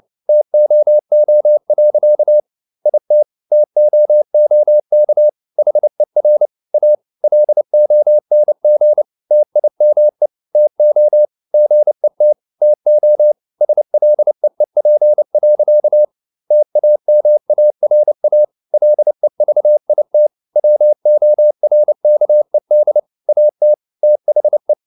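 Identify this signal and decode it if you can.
Morse code, heard as a keyed tone, about 29 wpm, 600 Hz. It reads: TOO. IT TOOK HER A LONG TIME TO GET TO SLEEP. TAMARA LEVIT WORKED AT THE
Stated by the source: Morse code at 29 wpm, sending Ken Follett's novel Never in English, from CW med Gnister.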